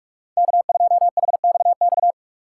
Morse code, heard as a keyed tone, about 44 wpm, 700 Hz. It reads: K2HXX